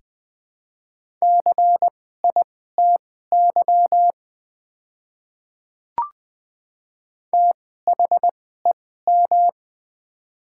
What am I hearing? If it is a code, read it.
Morse code, heard as a keyed tone, about 20 wpm, spaced as for 14 wpm, 700 Hz. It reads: CITY THEM
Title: 2x Character Spacing - Top 500 Words - 20wpm